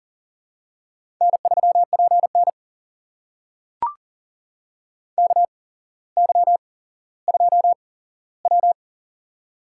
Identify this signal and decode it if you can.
Morse code, heard as a keyed tone, about 40 wpm, spaced as for 17 wpm, 700 Hz. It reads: N3PN KY2W
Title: USA Call Signs - 4 Characters - 8x Character Spacing - 40wpm